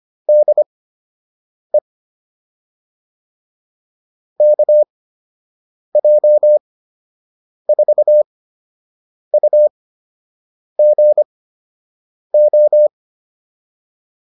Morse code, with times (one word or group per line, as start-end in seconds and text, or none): 0.3-1.8 DE
4.4-12.9 KJ4UGO